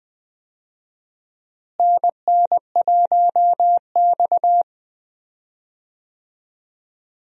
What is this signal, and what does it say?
Morse code, heard as a keyed tone, about 20 wpm, 700 Hz. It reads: NN1X
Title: USA Call Signs - 4 Characters - Repeated 2x - 20wpm